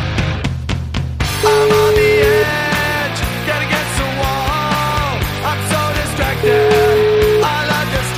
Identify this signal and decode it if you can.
Ringback tone (Europe, 425 Hz), with noise.